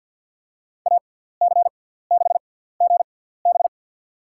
Morse code, heard as a keyed tone, about 53 wpm, spaced as for 17 wpm, 700 Hz. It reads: AC6GB
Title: USA Call Signs - 5 Characters - 6x Character Spacing - 50wpm